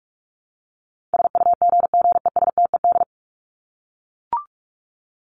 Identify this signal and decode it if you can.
Morse code, heard as a keyed tone, about 44 wpm, 700 Hz. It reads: SUGGESTED
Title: ED Suffix - Top 100 Words with Base Words 45wpm